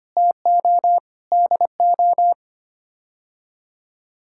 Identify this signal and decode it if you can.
Morse code, heard as a keyed tone, about 25 wpm, 700 Hz. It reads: TO DO
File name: Sets of 9 Words - 25wpm